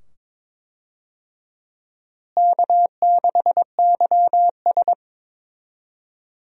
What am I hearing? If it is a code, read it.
Morse code, heard as a keyed tone, about 22 wpm, 700 Hz. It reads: K6YS
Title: USA Call Signs - 4 Characters - Repeated 2x - 22wpm